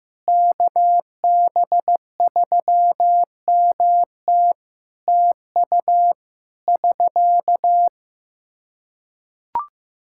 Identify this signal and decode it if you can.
Morse code, heard as a keyed tone, about 15 wpm, 700 Hz. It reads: KB3MT TU <SK>
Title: Practice QSO - Exchange 4 - Ending - 15wpm